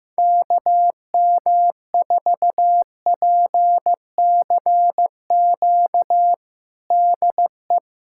Morse code, heard as a keyed tone, about 15 wpm, 700 Hz. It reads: KM4PCQ DE
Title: USA Callsign DE Callsign - 6 Characters - 15wpm